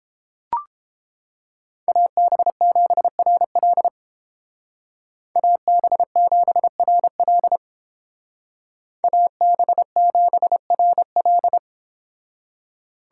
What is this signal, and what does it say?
Morse code, heard as a keyed tone, about 31 wpm, 700 Hz. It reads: AB7RL AB7RL AB7RL